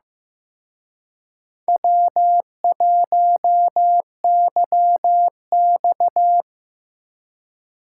Morse code, heard as a keyed tone, about 17 wpm, 700 Hz. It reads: W1YX
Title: Speed Racing - USA 1x2 Call Signs 15wpm